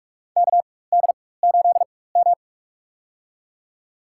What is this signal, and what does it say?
Morse code, heard as a keyed tone, about 44 wpm, spaced as for 21 wpm, 700 Hz. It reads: KD8M
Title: USA Call Signs - 4 Characters - 4x Character Spacing - 45wpm